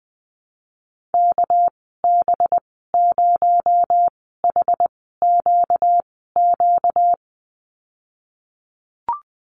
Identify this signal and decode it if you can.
Morse code, heard as a keyed tone, about 20 wpm, 700 Hz. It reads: K B 0 H Q Q
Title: 238 Lesson 38g - Letter q - Call Signs 20wpm